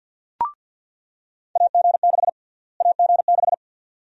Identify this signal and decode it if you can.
Morse code, heard as a keyed tone, about 47 wpm, 700 Hz. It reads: AG6 AG6
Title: USA Call Sign Prefixes - Repeated 2x - 50wpm